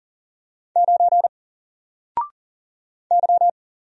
Morse code, heard as a keyed tone, about 39 wpm, 700 Hz. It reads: 9 Y
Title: Single Letter-Number 40wpm